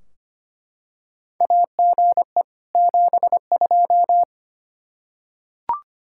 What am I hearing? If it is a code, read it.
Morse code, heard as a keyed tone, about 25 wpm, 700 Hz. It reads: AGE 72
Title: K1USN SST QSO Practice 25wpm